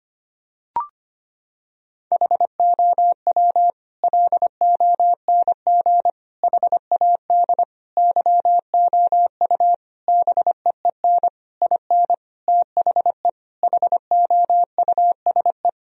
Morse code, heard as a keyed tone, about 25 wpm, 700 Hz. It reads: HOW LONG HAD YOU BEEN IN THE HOUSE
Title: Sentences from Top 1000 Words - Review Part 3 - 25wpm